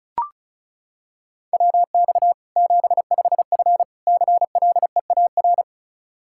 Morse code, heard as a keyed tone, about 34 wpm, 700 Hz. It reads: WX 75F CLEAR
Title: K1USN SST QSO Practice 35wpm